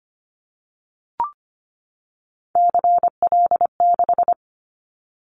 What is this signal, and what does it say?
Morse code, heard as a keyed tone, about 25 wpm, 700 Hz. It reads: CL6
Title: Callsign Prefixes for Top 100 DXCC - 25wpm